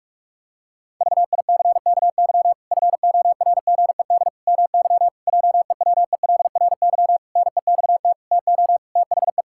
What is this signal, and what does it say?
Morse code, heard as a keyed tone, about 46 wpm, 700 Hz. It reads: VICKY FORGED MY JEWELRY NEXT TO THE
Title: Pangram Challenge v2 - 45wpm